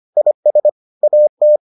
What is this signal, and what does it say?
Morse code, heard as a keyed tone, about 25 wpm, 600 Hz. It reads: IS AT